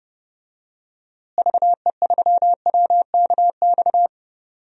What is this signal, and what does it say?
Morse code, heard as a keyed tone, about 30 wpm, 700 Hz. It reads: VE3WKX
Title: DXCC-100 Call Signs - 6 Characters - Repeated 3x - 30wpm